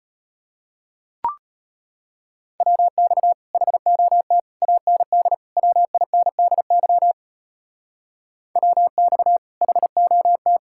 Morse code, heard as a keyed tone, about 31 wpm, 700 Hz. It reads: WX HOT AND WINDY WX HOT
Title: Speed-Racing Common QSO Phrases - 25wpm